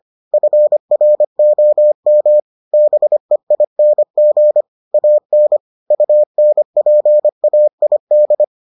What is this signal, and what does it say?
Morse code, heard as a keyed tone, about 25 wpm, 600 Hz. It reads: FROM BEING AN UNPAID